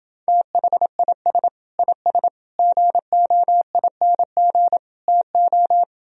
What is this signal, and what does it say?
Morse code, heard as a keyed tone, about 29 wpm, 700 Hz. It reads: THIS IS GOING TO